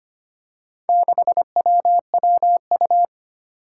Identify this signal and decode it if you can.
Morse code, heard as a keyed tone, about 25 wpm, 700 Hz. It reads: 6WWU
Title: Four Challenging Letter-Number Combinations 25wpm